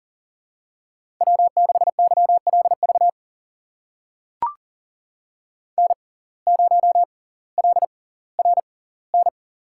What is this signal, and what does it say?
Morse code, heard as a keyed tone, about 40 wpm, spaced as for 20 wpm, 700 Hz. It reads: W6YLV N0LRN
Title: USA Call Signs - 5 Characters - 6x Character Spacing - 40wpm